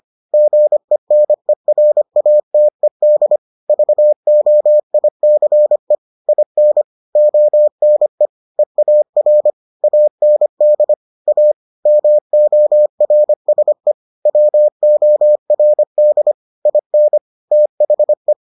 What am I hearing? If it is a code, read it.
Morse code, heard as a keyed tone, about 25 wpm, 600 Hz. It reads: GENERATED VOICE IN ONE EAR AND A MORSE WORD IN THE